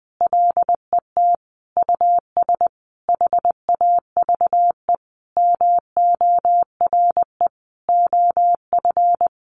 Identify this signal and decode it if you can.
Morse code, heard as a keyed tone, about 19 wpm, 700 Hz. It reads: LET US HAVE MORE OF
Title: Sentences from Top 1300 Words 20wpm